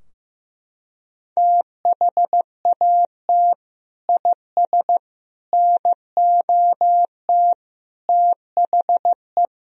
Morse code, heard as a keyed tone, about 16 wpm, 700 Hz. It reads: THAT IS NOT THE
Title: Sentences from Top 1000 Words - Review Part 2 - 15wpm